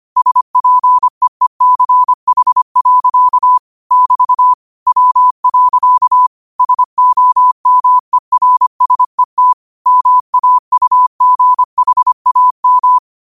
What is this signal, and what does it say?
Morse code, heard as a keyed tone, about 25 wpm, 1000 Hz. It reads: SPEECH. <BT> W. SOMERSET MAUGHAM